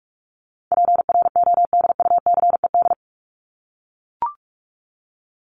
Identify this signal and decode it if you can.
Morse code, heard as a keyed tone, about 44 wpm, 700 Hz. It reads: PRODUCED
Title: ED Suffix - Top 100 Words with Base Words 45wpm